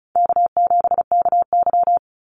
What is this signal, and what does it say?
Morse code, heard as a keyed tone, about 34 wpm, 700 Hz. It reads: K7KY
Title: Easy Call Signs Repeated 2x - 35wpm